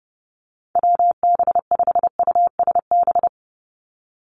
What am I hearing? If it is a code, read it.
Morse code, heard as a keyed tone, about 30 wpm, 700 Hz. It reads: WB5USB
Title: NA SOTA Chaser Callsigns 30wpm